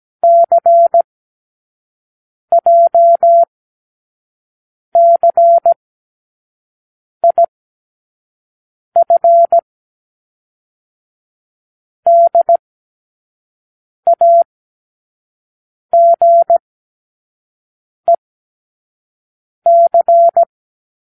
Morse code, heard as a keyed tone, about 17 wpm, spaced as for 5 wpm, 680 Hz. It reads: CJCIF DAGEC
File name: k7qo / Lesson15(A-J)